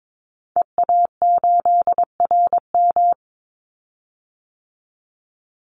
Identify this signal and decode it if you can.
Morse code, heard as a keyed tone, about 22 wpm, 700 Hz. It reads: EA8RM
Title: Call Signs Repeated 2x - 22wpm